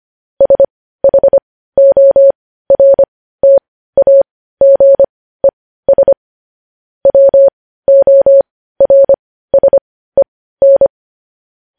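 Morse code, heard as a keyed tone, about 25 wpm, spaced as for 15 wpm, 550 Hz. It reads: SHORTAGES WORSEN
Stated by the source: Morse code practice of news headlines at 15 wpm, opening with vvv.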